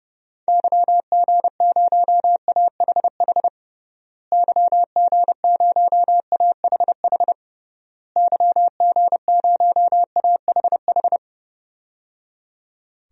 Morse code, heard as a keyed tone, about 30 wpm, 700 Hz. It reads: YG0AHH YG0AHH YG0AHH
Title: DXCC-100 Call Signs - 6 Characters - Repeated 3x - 30wpm